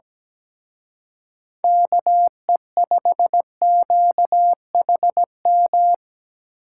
Morse code, heard as a keyed tone, about 19 wpm, 700 Hz. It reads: KE5QHM
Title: Speed Racing - USA 2x3 Call Signs 17wpm